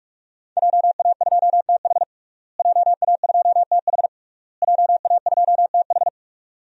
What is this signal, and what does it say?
Morse code, heard as a keyed tone, about 46 wpm, 700 Hz. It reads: JA2TH JA2TH JA2TH
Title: DXCC-100 Call Signs - 5 Characters - Repeated 3x - 45wpm